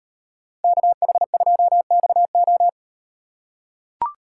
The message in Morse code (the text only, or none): KH2XO